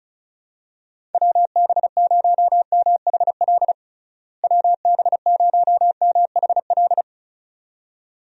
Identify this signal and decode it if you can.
Morse code, heard as a keyed tone, about 34 wpm, 700 Hz. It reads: WB0MHL WB0MHL
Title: USA Call Signs - 6 Characters - Repeated 2x - 35wpm